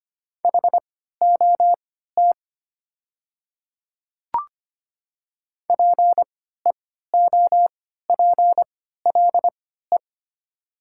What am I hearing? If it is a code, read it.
Morse code, heard as a keyed tone, about 25 wpm, spaced as for 14 wpm, 700 Hz. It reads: HOT PEOPLE
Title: 3x Character Spacing - Top 300 Words - 25wpm